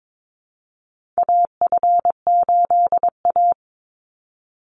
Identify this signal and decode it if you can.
Morse code, heard as a keyed tone, about 22 wpm, 700 Hz. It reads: AF8A
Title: Call Signs - 22wpm